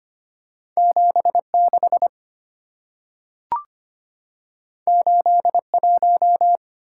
Morse code, heard as a keyed tone, about 25 wpm, 690 Hz. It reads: 76 81
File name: Two Letter-Number Combinations - 25wpm